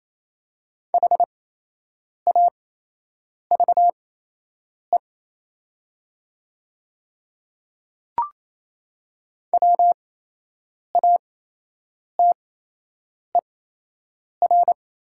Morse code, heard as a keyed tone, about 27 wpm, spaced as for 8 wpm, 700 Hz. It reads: HAVE WATER
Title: 8x Character Spacing - Top 200 Words - 28wpm